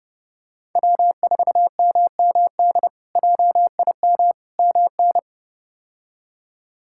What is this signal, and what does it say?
Morse code, heard as a keyed tone, about 30 wpm, 700 Hz. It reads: W4MMD JIM MN
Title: K1USN SST QSO Practice - 30wpm